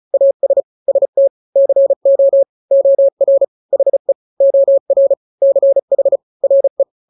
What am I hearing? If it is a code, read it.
Morse code, heard as a keyed tone, about 35 wpm, 550 Hz. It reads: AS ST CO OR HE OR CH RE
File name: n-grams / 2-grams 35 30 WPM